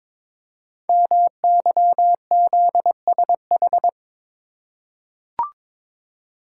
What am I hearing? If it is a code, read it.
Morse code, heard as a keyed tone, about 22 wpm, 700 Hz. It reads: MYZSH
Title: Code Groups - 5 Letter:Numbers - 22wpm